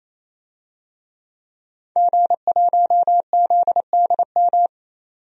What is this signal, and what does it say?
Morse code, heard as a keyed tone, about 28 wpm, 700 Hz. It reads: G1ZDM
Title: DXCC-100 Call Signs - 5 Characters - 28wpm